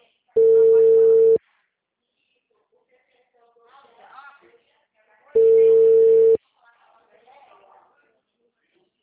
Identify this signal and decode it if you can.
Ringback tone (China, 450 Hz), with speech.
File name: out- -2012- - - .46815